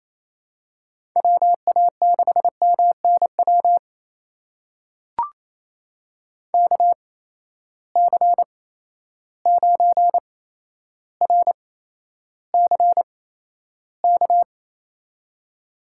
Morse code, heard as a keyed tone, about 28 wpm, spaced as for 12 wpm, 700 Hz. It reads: WA6MNW KC9RCK
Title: USA Call Signs - 6 Characters - 8x Character Spacing - 28wpm